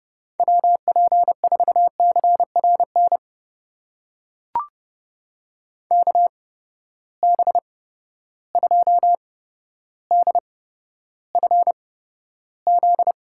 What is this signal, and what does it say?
Morse code, heard as a keyed tone, about 30 wpm, spaced as for 13 wpm, 700 Hz. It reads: WP4CRN KB2DFZ
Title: USA Call Signs - 6 Characters - 8x Character Spacing - 30wpm